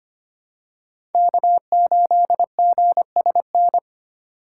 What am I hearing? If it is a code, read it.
Morse code, heard as a keyed tone, about 25 wpm, 700 Hz. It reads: K8GSN
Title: USA Call Signs - 4x Character Spacing - 25wpm